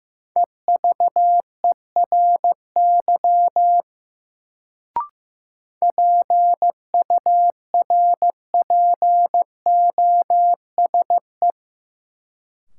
Morse code, heard as a keyed tone, about 15 wpm, 700 Hz. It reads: EVERY PURPOSE
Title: Sets of 5 Words - New Word Drill - 15wpm